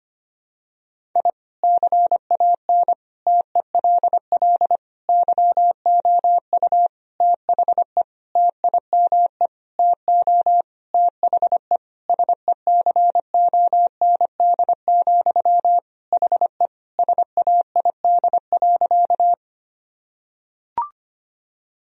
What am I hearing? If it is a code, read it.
Morse code, heard as a keyed tone, about 25 wpm, 700 Hz. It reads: I CAN TELL YOU THE TIME TO THE SECOND, HE SAID.
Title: Sentences from Top 1000 Words - Review Part 1 - 25wpm